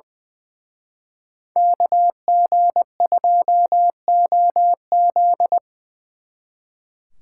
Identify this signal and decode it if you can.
Morse code, heard as a keyed tone, about 23 wpm, 700 Hz. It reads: KG2OZ
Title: Speed Racing - USA 2x2 Call Signs 20wpm